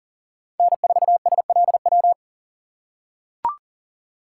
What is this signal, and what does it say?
Morse code, heard as a keyed tone, about 40 wpm, 700 Hz. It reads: N4SLW
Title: USA Call Signs - 5 Characters - 4x Character Spacing - 40wpm